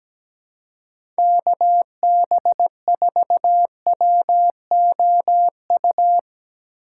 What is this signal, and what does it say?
Morse code, heard as a keyed tone, about 17 wpm, 700 Hz. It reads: KB4WOU
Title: USA Call Signs - 4x Character Spacing - 17wpm